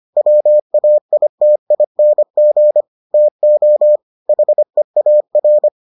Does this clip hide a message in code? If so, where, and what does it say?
Yes: Morse code from 0.2 to 5.7 s, heard as a keyed tone, about 25 wpm, 600 Hz. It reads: WAITING TO HEAR